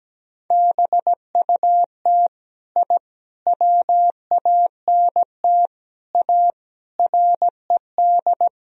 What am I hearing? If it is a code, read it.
Morse code, heard as a keyed tone, about 17 wpm, 700 Hz. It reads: BUT I WANT A RED
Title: Sentences from Top 1000 Words - Review Part 2 - 17wpm